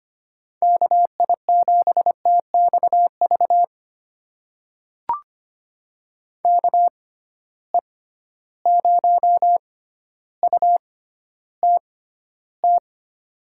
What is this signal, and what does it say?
Morse code, heard as a keyed tone, about 25 wpm, spaced as for 13 wpm, 700 Hz. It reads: KI7TXV KE0UTT
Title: USA Call Signs - 6 Characters - 6x Character Spacing - 25wpm